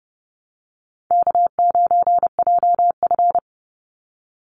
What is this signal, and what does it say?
Morse code, heard as a keyed tone, about 30 wpm, 700 Hz. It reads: K9JF
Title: Easy Call Signs - 30wpm